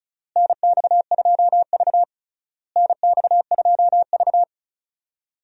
Morse code, heard as a keyed tone, about 36 wpm, 700 Hz. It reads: NX2V NX2V
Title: USA Call Signs - 4 Characters - Repeated 3x - 35wpm